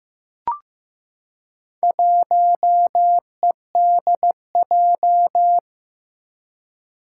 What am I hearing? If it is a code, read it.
Morse code, heard as a keyed tone, about 15 wpm, 700 Hz. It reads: 1EDJ
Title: Four Challenging Letter-Number Combinations 15wpm